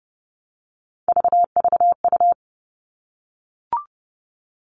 Morse code, heard as a keyed tone, about 30 wpm, 700 Hz. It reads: VVU
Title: All Cut Numbers - 3-Digit 30wpm